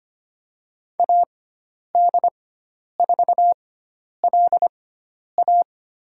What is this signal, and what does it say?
Morse code, heard as a keyed tone, about 25 wpm, spaced as for 10 wpm, 700 Hz. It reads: AD4LA
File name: Lesson 16g - Letter c - Call Signs 25wpm